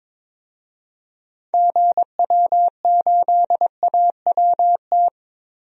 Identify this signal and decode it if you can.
Morse code, heard as a keyed tone, about 22 wpm, 700 Hz. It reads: GW8AWT
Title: DXCC-100 Call Signs - 6 Characters - 22wpm